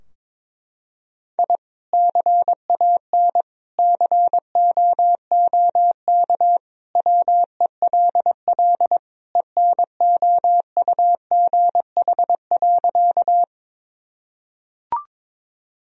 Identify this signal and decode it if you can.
Morse code, heard as a keyed tone, about 22 wpm, 700 Hz. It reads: I CAN COOK WELL ENOUGH.
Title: Sentences from Top 1000 Words - Review Part 4 - 22wpm